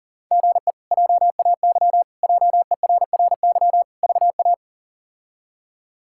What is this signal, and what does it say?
Morse code, heard as a keyed tone, about 40 wpm, 700 Hz. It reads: GE JAY JERRY VA
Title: K1USN SST QSO Practice - 40wpm